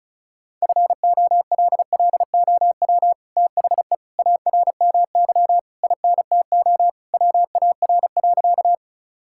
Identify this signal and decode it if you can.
Morse code, heard as a keyed tone, about 34 wpm, 700 Hz. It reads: FOLLOW THE ARMY INTO WAR.
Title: Sentences from Top 1000 Words - Review Part 5 - 35wpm